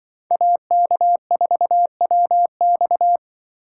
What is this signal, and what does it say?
Morse code, heard as a keyed tone, about 25 wpm, 700 Hz. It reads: AK4WX
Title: Speed Racing - USA 2x2 Call Signs 20wpm